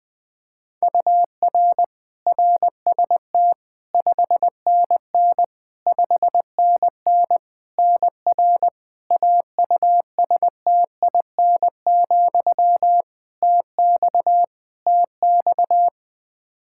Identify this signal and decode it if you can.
Morse code, heard as a keyed tone, about 20 wpm, 700 Hz. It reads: UR RST 5NN 5NN NR AUSTIN, TX TX
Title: Practice QSO - Exchange 1 - Beginning - 20wpm